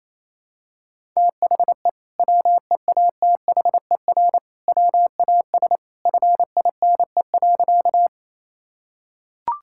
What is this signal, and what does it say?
Morse code, heard as a keyed tone, about 28 wpm, 700 Hz. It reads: THE WEATHER WAS FINE.
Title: Sentences from Top 1000 Words - Review Part 5 - 28wpm